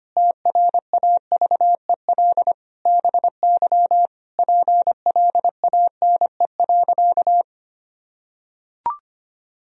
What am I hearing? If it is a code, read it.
Morse code, heard as a keyed tone, about 25 wpm, 700 Hz. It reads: TRAVEL BY PLANE.
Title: Sentences from Top 1000 Words - Review Part 3 - 25wpm